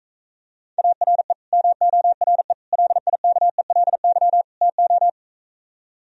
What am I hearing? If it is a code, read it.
Morse code, heard as a keyed tone, about 45 wpm, 700 Hz. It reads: ARE MORE LIKELY TO